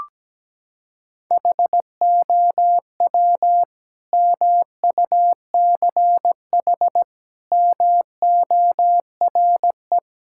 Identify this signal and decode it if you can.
Morse code, heard as a keyed tone, about 17 wpm, 700 Hz. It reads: HOW MUCH MORE
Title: Sentences from Top 1000 Words - Review Part 5 - 17wpm